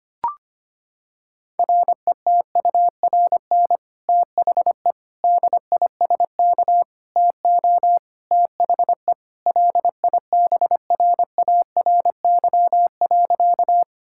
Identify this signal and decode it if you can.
Morse code, heard as a keyed tone, about 25 wpm, 700 Hz. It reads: RETURN THE DISK TO THE LIBRARY.
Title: Sentences from Top 1000 Words - Review Part 6 - 25wpm